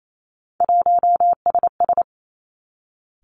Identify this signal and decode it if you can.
Morse code, heard as a keyed tone, about 28 wpm, 700 Hz. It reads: 1SS
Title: Three Challenging Letter-Number Combinations 28wpm